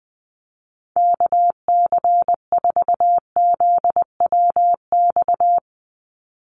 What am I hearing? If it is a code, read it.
Morse code, heard as a keyed tone, about 20 wpm, 700 Hz. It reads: KC4ZWX
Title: USA Call Signs 20wpm